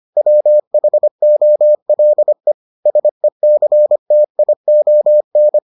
Morse code, heard as a keyed tone, about 25 wpm, 600 Hz. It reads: WHOLE SECTION